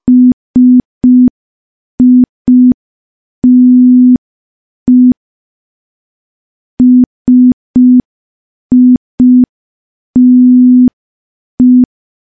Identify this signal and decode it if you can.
Morse code, heard as a keyed tone, about 5 wpm, 250 Hz. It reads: SITE SITE
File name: morse / morse-training-05WPM